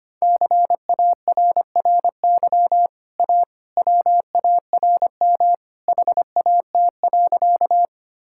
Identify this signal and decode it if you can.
Morse code, heard as a keyed tone, about 25 wpm, 700 Hz. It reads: CARRY A WARM HAT.
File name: Easy Sentences with Hints - 25wpm